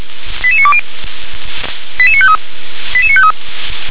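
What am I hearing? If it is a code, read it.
Five-tone selective call (ZVEI), heard as quick runs of five tones, tones of about 70 ms. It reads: 88920 88953 88953
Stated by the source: ZVEI five-tone selective call received by SDR.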